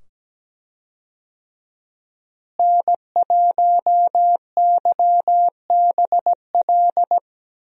Morse code, heard as a keyed tone, about 17 wpm, 700 Hz. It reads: N1YBL
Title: USA Call Signs - 5 Characters - Repeated 2x - 17wpm